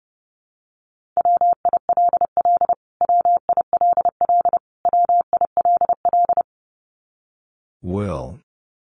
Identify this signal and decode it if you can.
Morse code, heard as a keyed tone, about 30 wpm, 700 Hz. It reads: WILL WILL WILL
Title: Top 200 Words Repeated 30wpm